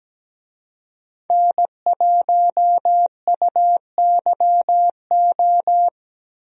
Morse code, heard as a keyed tone, about 17 wpm, 700 Hz. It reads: N1UYO
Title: USA Call Signs - 5 Characters - Repeated 3x - 17wpm